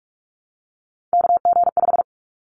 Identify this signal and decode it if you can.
Morse code, heard as a keyed tone, about 44 wpm, 700 Hz. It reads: KG5
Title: Callsign Prefixes for Top 25 DXCC - Repeated 3x - 45wpm